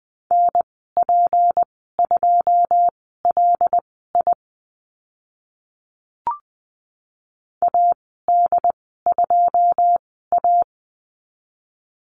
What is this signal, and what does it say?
Morse code, heard as a keyed tone, about 20 wpm, spaced as for 14 wpm, 700 Hz. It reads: NP2LI AD2A
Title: 161 Lesson 26g - Letter g - Call Signs 20wpm